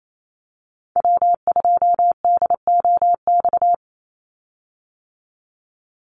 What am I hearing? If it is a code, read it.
Morse code, heard as a keyed tone, about 28 wpm, 700 Hz. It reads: W2DOX
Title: USA Call Signs 28wpm